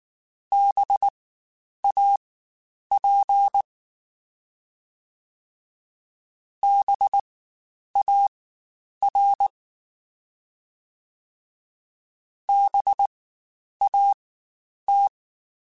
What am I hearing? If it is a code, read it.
Morse code, heard as a keyed tone, about 19 wpm, spaced as for 9 wpm, 800 Hz. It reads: BAP BAR BAT
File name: Spaced Alphas / Spaced Alphas mp3s / Bs sp @18